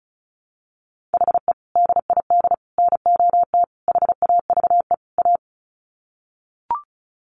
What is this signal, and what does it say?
Morse code, heard as a keyed tone, about 34 wpm, 700 Hz. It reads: HE DID NOT HAVE A
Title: Sets of 5 Words Encore 35wpm